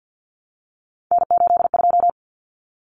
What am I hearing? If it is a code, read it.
Morse code, heard as a keyed tone, about 50 wpm, 700 Hz. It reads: N82
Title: Common Cut Numbers - 3-Digit 50wpm